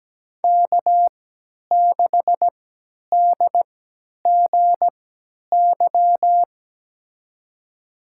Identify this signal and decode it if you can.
Morse code, heard as a keyed tone, about 17 wpm, spaced as for 10 wpm, 700 Hz. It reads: K6DGY
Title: USA Call Signs - 5 Characters - 3x Character Spacing - 17wpm